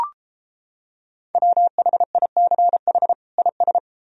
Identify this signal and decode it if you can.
Morse code, heard as a keyed tone, about 33 wpm, 700 Hz. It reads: WHICH IS